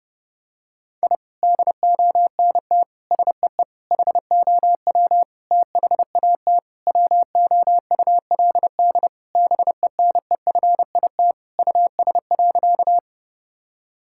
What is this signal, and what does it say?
Morse code, heard as a keyed tone, about 32 wpm, 700 Hz. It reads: I DONT SEE HOW THAT WOULD BENEFIT US.
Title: Sentences from Top 1000 Words - Review Part 6 - 30wpm